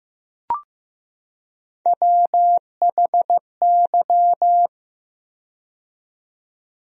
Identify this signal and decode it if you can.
Morse code, heard as a keyed tone, about 15 wpm, 700 Hz. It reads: WHY
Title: 4x Word-Spacing - Sentences from Top 200 Words - 15wpm